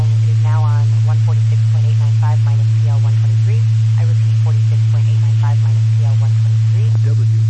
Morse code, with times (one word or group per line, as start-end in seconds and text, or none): none